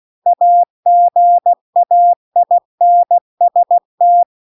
Morse code, heard as a keyed tone, about 16 wpm, 700 Hz. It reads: AGAINST